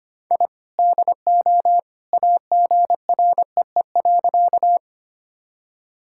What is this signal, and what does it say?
Morse code, heard as a keyed tone, about 25 wpm, 700 Hz. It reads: I DO AGREE.
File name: practice / 25wpm / Sentences with 2-Word Phrases - 25wpm